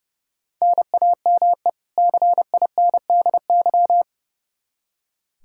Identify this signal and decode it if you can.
Morse code, heard as a keyed tone, about 30 wpm, 700 Hz. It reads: NAME CINDY
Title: K1USN SST QSO Practice 30wpm